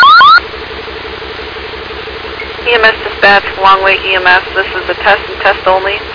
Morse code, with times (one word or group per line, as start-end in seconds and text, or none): none